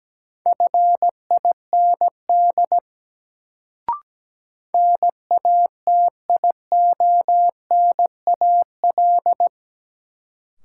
Morse code, heard as a keyed tone, about 17 wpm, 700 Hz. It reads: FIND NATIONAL
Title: Sets of 2 Words - New Word Drill - 17wpm